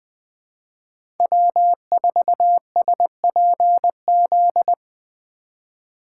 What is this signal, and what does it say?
Morse code, heard as a keyed tone, about 20 wpm, 700 Hz. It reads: W4SPZ
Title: USA Call Signs - 4x Character Spacing - 20wpm